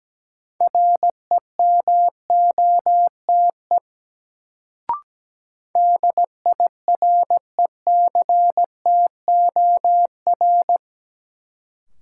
Morse code, heard as a keyed tone, about 17 wpm, 700 Hz. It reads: REMOTE DIRECTOR
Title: Sentences from Top 1300 Words - New Word Drill - 17wpm